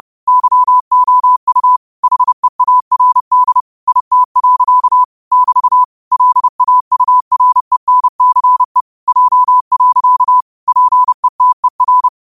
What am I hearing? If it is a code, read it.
Morse code, heard as a keyed tone, about 30 wpm, 1000 Hz. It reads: YOU HEARD IT. <BT> LAURENCE J. PETER